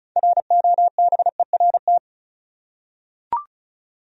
Morse code, heard as a keyed tone, about 36 wpm, 700 Hz. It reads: ROBERT
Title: USA Common Names - 35wpm